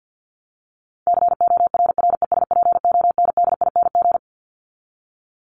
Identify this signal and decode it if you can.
Morse code, heard as a keyed tone, about 50 wpm, 700 Hz. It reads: CORRESPONDING
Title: ING Suffix - Top 100 Words with Base Words 50wpm